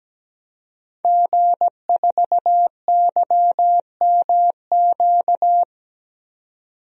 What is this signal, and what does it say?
Morse code, heard as a keyed tone, about 17 wpm, 700 Hz. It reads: G4YMQ
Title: DXCC-100 Call Signs - 5 Characters - 17wpm